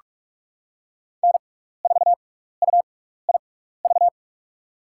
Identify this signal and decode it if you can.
Morse code, heard as a keyed tone, about 46 wpm, spaced as for 15 wpm, 690 Hz. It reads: N4UIV
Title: USA Call Signs - 5 Characters - 6x Character Spacing - 45wpm